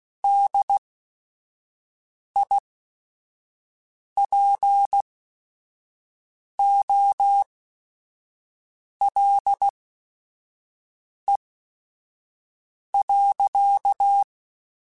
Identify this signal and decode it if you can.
Morse code, heard as a keyed tone, about 16 wpm, spaced as for 5 wpm, 800 Hz. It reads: DIPOLE.
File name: QSOs / QSOs-mp3s / WD0FBT 1@5